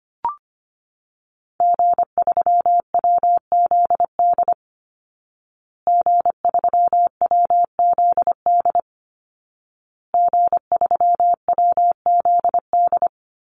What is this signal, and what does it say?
Morse code, heard as a keyed tone, about 25 wpm, 700 Hz. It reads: G3WZD G3WZD G3WZD
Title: Call Signs Repeated 2x - 25wpm